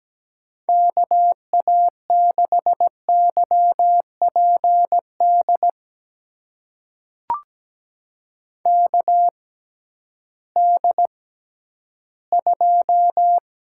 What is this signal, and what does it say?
Morse code, heard as a keyed tone, about 17 wpm, spaced as for 11 wpm, 700 Hz. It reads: KA6YPD KD2
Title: USA Call Signs - 6 Characters - 6x Character Spacing - 17wpm